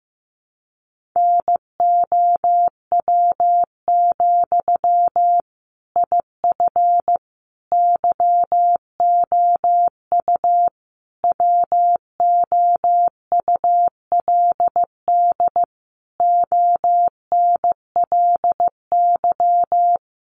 Morse code, heard as a keyed tone, about 15 wpm, 700 Hz. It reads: NOW, IF YOU WOULD ONLY
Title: Sentences from Top 900 Words 15wpm